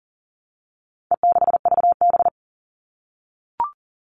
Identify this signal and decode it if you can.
Morse code, heard as a keyed tone, about 40 wpm, 700 Hz. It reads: E6VB
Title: All Cut Numbers - 4-Digit 40wpm